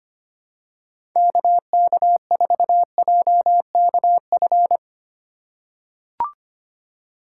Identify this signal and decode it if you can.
Morse code, heard as a keyed tone, about 25 wpm, 700 Hz. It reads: KK4JKF
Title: USA Call Signs - 6 Characters - 2x Character Spacing - 25wpm